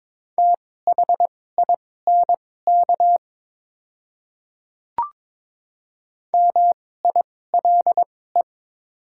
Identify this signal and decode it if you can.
Morse code, heard as a keyed tone, about 22 wpm, spaced as for 16 wpm, 700 Hz. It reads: THINK MILE
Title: 2x Character Spacing - Top 300 Words - 22wpm